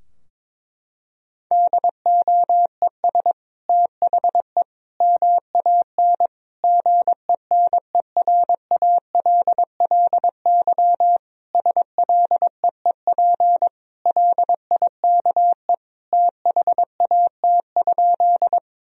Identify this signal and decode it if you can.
Morse code, heard as a keyed tone, about 21 wpm, 700 Hz. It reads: DOES THE MAN GENERALLY SLEEP LIKE THAT?
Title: Sentences from Top 1000 Words - Review Part 4 - 22wpm